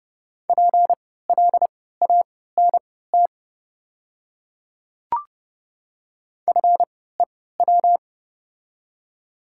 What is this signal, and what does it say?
Morse code, heard as a keyed tone, about 30 wpm, spaced as for 17 wpm, 700 Hz. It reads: PLANT FEW
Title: 3x Character Spacing - Top 300 Words - 30wpm